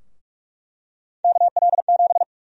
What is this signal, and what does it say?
Morse code, heard as a keyed tone, about 47 wpm, 700 Hz. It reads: KL7
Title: USA Call Sign Prefixes - 45wpm